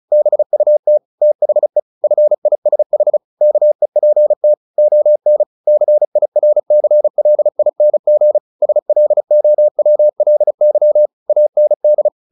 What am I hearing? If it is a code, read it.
Morse code, heard as a keyed tone, about 34 wpm, 600 Hz. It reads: BUT THE FISH KEPT ON CIRCLING SLOWLY AND
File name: the old man and the sea 2 35wpm